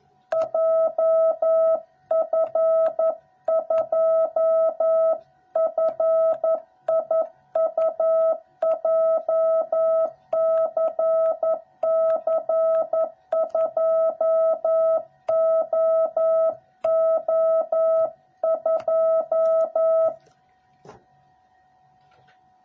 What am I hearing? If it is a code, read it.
Morse code, heard as a keyed tone, about 11 wpm, 660 Hz. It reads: JF2FIU JCC2OO2
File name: jh8cbh / 2021cw / cw06b